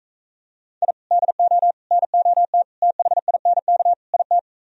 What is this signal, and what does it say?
Morse code, heard as a keyed tone, about 45 wpm, 700 Hz. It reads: I DO NOT THINK IT